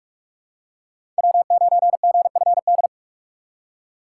Morse code, heard as a keyed tone, about 46 wpm, 700 Hz. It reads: W9GFD
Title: USA Call Signs - 6x Character Spacing - 45wpm